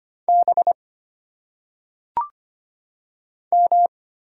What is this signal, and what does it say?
Morse code, heard as a keyed tone, about 25 wpm, 700 Hz. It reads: B M
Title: Single Letters - Phonetic Pronunciation 25wpm